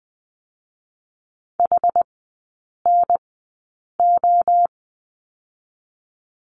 Morse code, heard as a keyed tone, about 20 wpm, spaced as for 8 wpm, 700 Hz. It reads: HNO
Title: 060 Lesson 11e - Letter h - Three Characters 20wpm